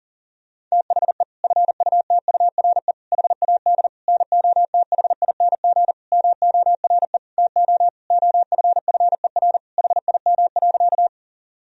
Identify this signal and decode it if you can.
Morse code, heard as a keyed tone, about 40 wpm, 700 Hz. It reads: THE FUTURE HAD NOTHING MORE TO OFFER HIM.
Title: Sentences from Top 1000 Words - Review Part 4 - 40wpm